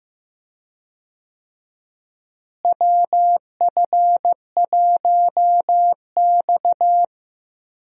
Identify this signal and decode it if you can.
Morse code, heard as a keyed tone, about 15 wpm, 700 Hz. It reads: WF1X
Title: USA Call Signs - 4 Characters - Repeated 2x - 15wpm